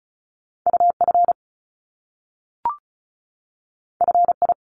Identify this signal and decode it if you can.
Morse code, heard as a keyed tone, about 36 wpm, 700 Hz. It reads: UF FI
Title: Top Two Letter Combinations in Words 35wpm